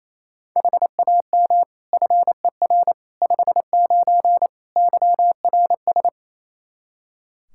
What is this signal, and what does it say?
Morse code, heard as a keyed tone, about 28 wpm, 700 Hz. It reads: HAM FER 59 YRS
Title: Practice QSO - Exchange 3 - More Info - 28wpm